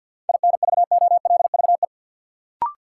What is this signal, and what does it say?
Morse code, heard as a keyed tone, about 53 wpm, 700 Hz. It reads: INVOLVE
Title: Sentences from Top 1200 Words - New Word Drill Alt - 50wpm